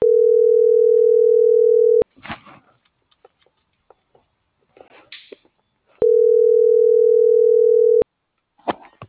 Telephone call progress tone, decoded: ringback tone

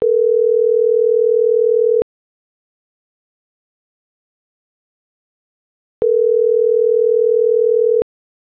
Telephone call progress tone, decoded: ringback tone

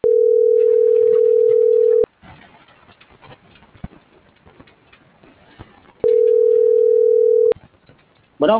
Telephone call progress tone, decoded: ringback tone

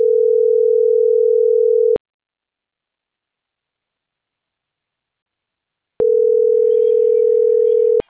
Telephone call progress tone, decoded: ringback tone

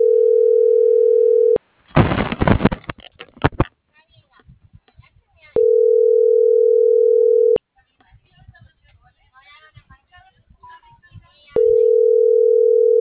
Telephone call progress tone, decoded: ringback tone

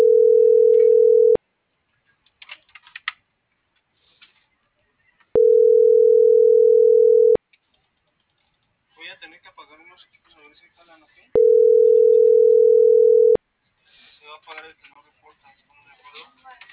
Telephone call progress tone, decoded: ringback tone

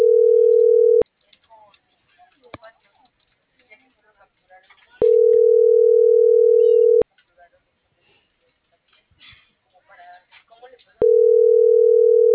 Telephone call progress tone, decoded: ringback tone

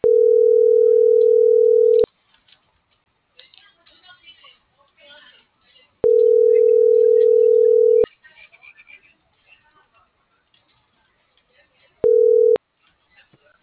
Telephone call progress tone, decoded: ringback tone

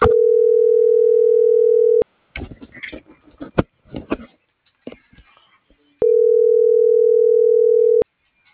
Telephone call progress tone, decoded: ringback tone

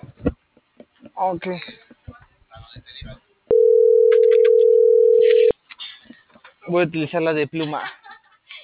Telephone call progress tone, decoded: ringback tone

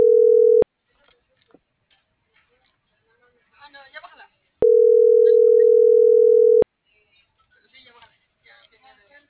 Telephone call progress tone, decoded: ringback tone

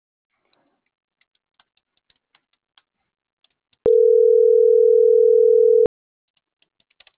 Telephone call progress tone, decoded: ringback tone

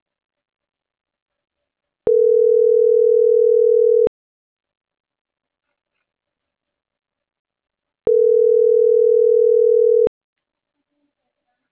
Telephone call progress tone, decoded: ringback tone

